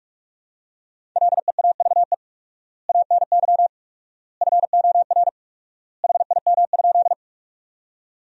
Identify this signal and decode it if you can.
Morse code, heard as a keyed tone, about 48 wpm, 700 Hz. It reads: LEAVE ANY FOR HIM?
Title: 4x Word-Spacing - Sentences from Top 400 Words - 45wpm